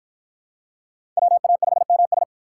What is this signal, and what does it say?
Morse code, heard as a keyed tone, about 53 wpm, 700 Hz. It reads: WA5MS